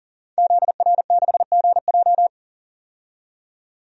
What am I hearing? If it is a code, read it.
Morse code, heard as a keyed tone, about 40 wpm, 700 Hz. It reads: ZR6GJ
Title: DXCC-100 Call Signs - 5 Characters - 40wpm